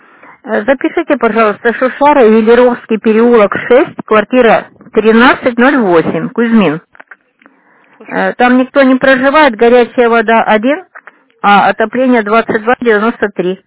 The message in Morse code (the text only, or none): none